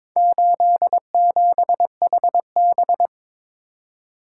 Morse code, heard as a keyed tone, about 22 wpm, 700 Hz. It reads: 87HB
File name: practice / 22wpm / Four Letter-Number Combinations - 22wpm